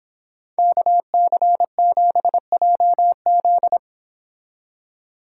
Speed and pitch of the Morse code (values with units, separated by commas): 29 wpm, 700 Hz